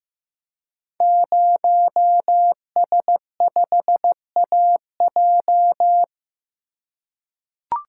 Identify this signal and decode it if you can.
Morse code, heard as a keyed tone, about 16 wpm, 700 Hz. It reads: 0S5AJ
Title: Code Groups - 5 Letter:Numbers - 15wpm